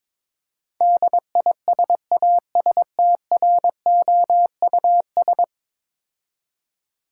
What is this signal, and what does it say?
Morse code, heard as a keyed tone, about 21 wpm, 700 Hz. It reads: DISASTROUS
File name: practice / 22wpm / 10 Letter Words - First 3 Letters Given - 22wpm